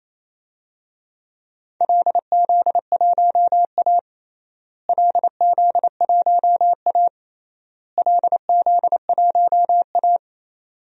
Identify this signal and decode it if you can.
Morse code, heard as a keyed tone, about 28 wpm, 700 Hz. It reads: LZ1A LZ1A LZ1A
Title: DXCC-100 Call Signs - 4 Characters - Repeated 3x - 28wpm